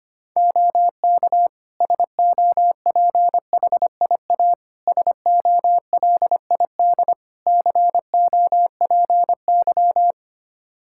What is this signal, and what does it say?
Morse code, heard as a keyed tone, about 25 wpm, 700 Hz. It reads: OK SOPHIA SOLID COPY
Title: Practice QSO - Exchange 3 - More Info - 25wpm